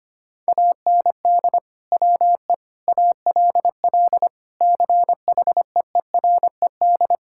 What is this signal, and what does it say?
Morse code, heard as a keyed tone, about 25 wpm, 700 Hz. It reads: AND WE ALL CHEERED
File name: practice / 25wpm / Sentences with 2-Word Phrases - 25wpm